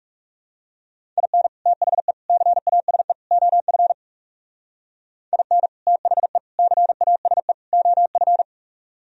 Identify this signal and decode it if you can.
Morse code, heard as a keyed tone, about 40 wpm, 690 Hz. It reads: IN THE CASE OF IN THE CASE OF